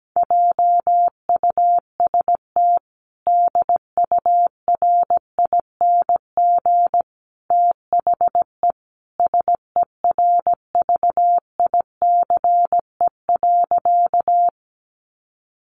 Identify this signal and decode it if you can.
Morse code, heard as a keyed tone, about 17 wpm, 700 Hz. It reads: JUST DURING THE SERVICE.
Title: Sentences from Top 700 Words 17wpm